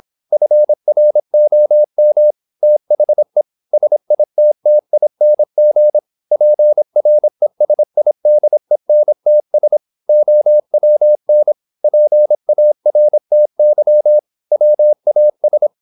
Morse code, heard as a keyed tone, about 26 wpm, 600 Hz. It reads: FROM THE SITTING PRESIDENTS OWN PARTY WAS
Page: Never 26wpm | CW med Gnister